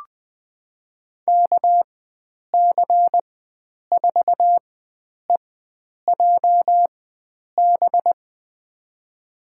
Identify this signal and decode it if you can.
Morse code, heard as a keyed tone, about 20 wpm, spaced as for 9 wpm, 700 Hz. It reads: KC4EJB
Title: USA Call Signs - 6 Characters - 4x Character Spacing - 20wpm